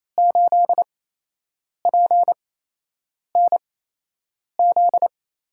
Morse code, heard as a keyed tone, about 28 wpm, spaced as for 8 wpm, 700 Hz. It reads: 8PNZ
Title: USA Call Signs - 5 Characters - 8x Character Spacing - 28wpm